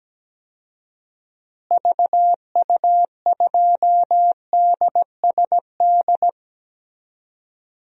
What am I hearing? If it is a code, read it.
Morse code, heard as a keyed tone, about 17 wpm, 700 Hz. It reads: VU2DSD
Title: DXCC-100 Call Signs - 6 Characters - Repeated 2x - 17wpm